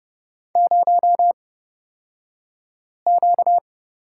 Morse code, heard as a keyed tone, about 31 wpm, spaced as for 5 wpm, 700 Hz. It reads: 0Q
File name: Lesson 38d - Letter q - Two Characters 30wpm